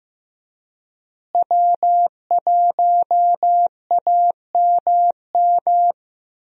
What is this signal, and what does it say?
Morse code, heard as a keyed tone, about 15 wpm, 700 Hz. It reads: W1AMM
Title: USA Call Signs - 5 Characters - Repeated 3x - 15wpm